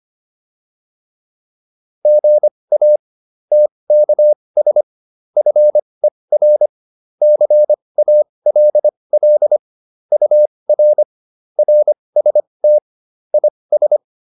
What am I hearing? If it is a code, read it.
Morse code, heard as a keyed tone, about 25 wpm, spaced as for 20 wpm, 600 Hz. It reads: GA TKS FER CALL UR RST IS